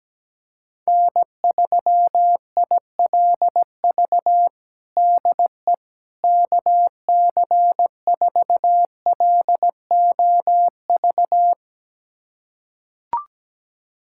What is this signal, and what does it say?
Morse code, heard as a keyed tone, about 17 wpm, 700 Hz. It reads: N3ILV DE KC4LOV
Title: USA Callsign DE Callsign 17wpm